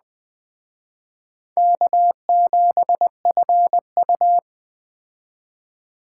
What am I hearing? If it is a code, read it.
Morse code, heard as a keyed tone, about 20 wpm, 700 Hz. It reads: K7FU